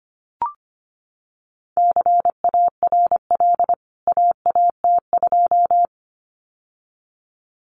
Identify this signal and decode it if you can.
Morse code, heard as a keyed tone, about 25 wpm, 700 Hz. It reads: CARL AAT2
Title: CWT Exchange 25wpm